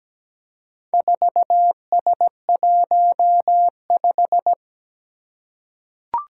Morse code, heard as a keyed tone, about 17 wpm, 700 Hz. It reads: 4S15
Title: Four Challenging Letter-Number Combinations 17wpm